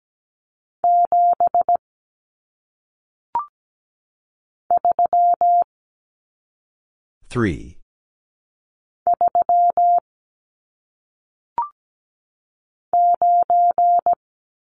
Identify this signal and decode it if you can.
Morse code, heard as a keyed tone, about 17 wpm, 700 Hz. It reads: 7 3 3 9
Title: Numbers - 1-Digit 17wpm